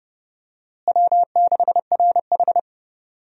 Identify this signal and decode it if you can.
Morse code, heard as a keyed tone, about 30 wpm, 700 Hz. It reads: W6RH
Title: USA Call Signs - 4 Characters - Repeated 3x - 30wpm